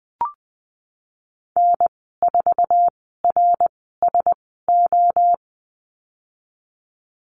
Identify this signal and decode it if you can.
Morse code, heard as a keyed tone, about 20 wpm, spaced as for 15 wpm, 700 Hz. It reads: N4RSO
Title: 069 Lesson 12g - Letter d - Call Signs 20wpm